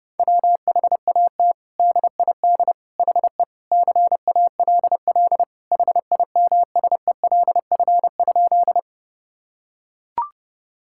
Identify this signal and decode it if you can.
Morse code, heard as a keyed tone, about 32 wpm, 700 Hz. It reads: WHAT DID HE CALL HIMSELF?
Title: Sentences from Top 1600 Words 30wpm